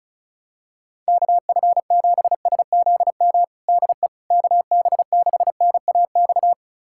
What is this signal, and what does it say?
Morse code, heard as a keyed tone, about 34 wpm, 700 Hz. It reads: KF7SZM DE KB6NAX
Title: USA Callsign DE Callsign - 6 Characters - 35wpm